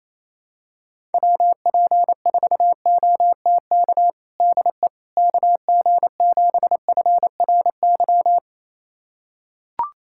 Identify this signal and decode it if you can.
Morse code, heard as a keyed tone, about 27 wpm, 700 Hz. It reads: WP4OTK DE KG7FRY
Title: USA Callsign DE Callsign - 6 Characters - 28wpm